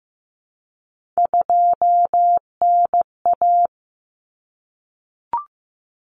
Morse code, heard as a keyed tone, about 15 wpm, 700 Hz. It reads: 2NA
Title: Common Cut Numbers - 3-Digit 15wpm